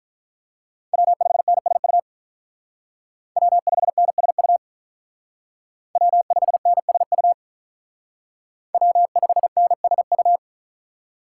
Morse code, heard as a keyed tone, about 45 wpm, 700 Hz. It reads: W5NSU W5NSU W5NSU W5NSU